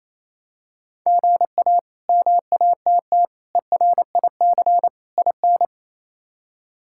Morse code, heard as a keyed tone, about 28 wpm, 700 Hz. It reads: GA MATT ERIC IN
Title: K1USN SST QSO Practice - 28wpm